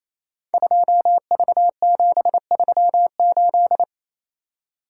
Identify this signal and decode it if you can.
Morse code, heard as a keyed tone, about 28 wpm, 700 Hz. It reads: 2V738
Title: Code Groups - 5 Letter:Numbers - 28wpm